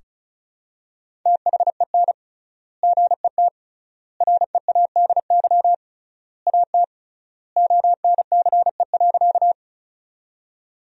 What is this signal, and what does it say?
Morse code, heard as a keyed tone, about 34 wpm, 700 Hz. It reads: THEN GET READY AT ONCE.
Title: 3x Word-Spacing - Sentences from Top 400 Words - 35wpm